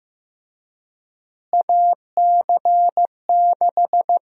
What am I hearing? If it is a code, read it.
Morse code, heard as a keyed tone, about 16 wpm, 700 Hz. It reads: AC6
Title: USA Call Sign Prefixes - Repeated 2x - 15wpm